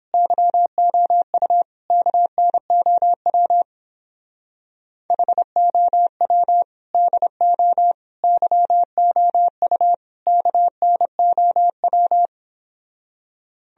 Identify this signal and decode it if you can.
Morse code, heard as a keyed tone, about 31 wpm, 700 Hz. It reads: YOU KNOW HOW DO YOU KNOW